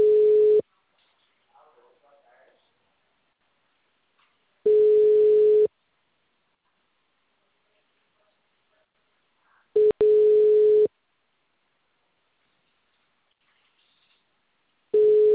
Ringback tone (Europe, 425 Hz).